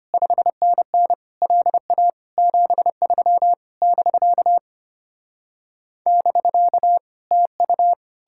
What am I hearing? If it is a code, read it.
Morse code, heard as a keyed tone, about 33 wpm, 700 Hz. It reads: 5NN LA 73 <BK> <BK> TU